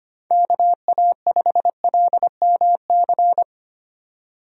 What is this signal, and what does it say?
Morse code, heard as a keyed tone, about 25 wpm, 700 Hz. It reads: KA5LMC
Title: USA Call Signs - 6 Characters - 25wpm